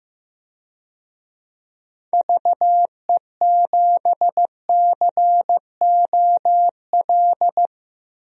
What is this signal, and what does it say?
Morse code, heard as a keyed tone, about 15 wpm, 700 Hz. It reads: VE7COL
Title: DXCC-100 Call Signs - 6 Characters - 15wpm